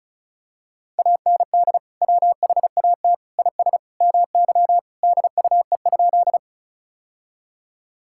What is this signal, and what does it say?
Morse code, heard as a keyed tone, about 36 wpm, 700 Hz. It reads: AND WHAT IS MY DUE?
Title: Sentences from Top 1000 Words - Review Part 6 - 35wpm